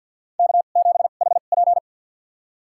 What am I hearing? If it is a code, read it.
Morse code, heard as a keyed tone, about 52 wpm, spaced as for 37 wpm, 700 Hz. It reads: K7HP
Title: USA Call Signs - 4 Characters - 2x Character Spacing - 50wpm